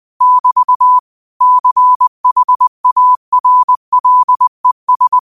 Morse code, heard as a keyed tone, about 20 wpm, 1000 Hz. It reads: <BT> CHARLES
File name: qotdfiles / 20wpm